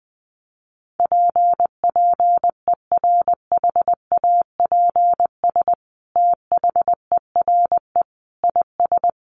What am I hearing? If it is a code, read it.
Morse code, heard as a keyed tone, about 20 wpm, 700 Hz. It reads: PPERHAPS THERE IS
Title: Sentences from Top 1100 Words 20wpm